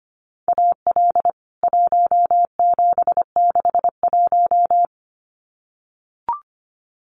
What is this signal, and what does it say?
Morse code, heard as a keyed tone, about 25 wpm, 700 Hz. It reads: AL 1761
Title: CWT Exchange 25wpm